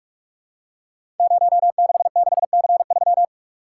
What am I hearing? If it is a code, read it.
Morse code, heard as a keyed tone, about 44 wpm, 700 Hz. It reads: 066C3
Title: Code Groups - 5 Letter:Numbers - 45wpm